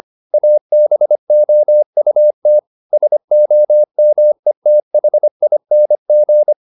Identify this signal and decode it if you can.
Morse code, heard as a keyed tone, about 25 wpm, 600 Hz. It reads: ABOUT SOMETHING